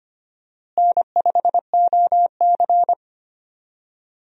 Morse code, heard as a keyed tone, about 25 wpm, 700 Hz. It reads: N5OC
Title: USA Call Signs - 4 Characters - Repeated 2x - 25wpm